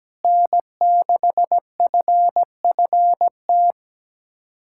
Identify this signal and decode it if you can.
Morse code, heard as a keyed tone, about 17 wpm, 700 Hz. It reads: N6FFT
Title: USA Callsign DE Callsign - 5 Characters - 17wpm